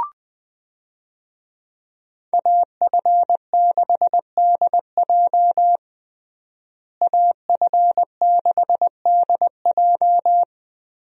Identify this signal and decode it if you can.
Morse code, heard as a keyed tone, about 20 wpm, 700 Hz. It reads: AF6DJ AF6DJ
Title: USA Call Signs - 5 Characters - Repeated 3x - 20wpm